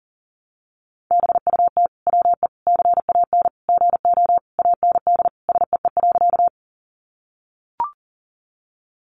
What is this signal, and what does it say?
Morse code, heard as a keyed tone, about 40 wpm, 700 Hz. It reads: BUT WE CAN GO AND SEE.
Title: Sentences from Top 100 Words 40wpm